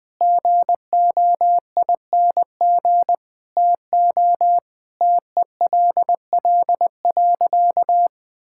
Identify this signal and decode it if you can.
Morse code, heard as a keyed tone, about 19 wpm, 700 Hz. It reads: GOING TO TELL.
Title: Sentences with 5-Word Phrases - 20wpm